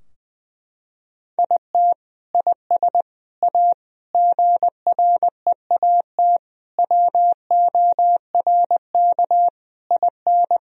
Morse code, heard as a keyed tone, about 20 wpm, 700 Hz. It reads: IT IS A GREAT WORK IN
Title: Sentences from Top 1000 Words - Review Part 3 - 20wpm